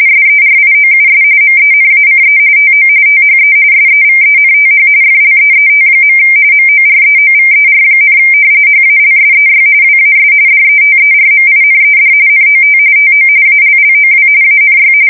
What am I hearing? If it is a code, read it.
RTTY, about 45.45 baud. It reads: SERIOUS MENACE TO THE SEVENTH FLEET. MOREOVER, HALSEY WAS DETERMINED NOT TO ALLOW ANY
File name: fsk170 / battleshipnj